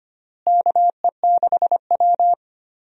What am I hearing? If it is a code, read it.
Morse code, heard as a keyed tone, about 25 wpm, 700 Hz. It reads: KE6W
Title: USA Call Signs - 4 Characters - Repeated 2x - 25wpm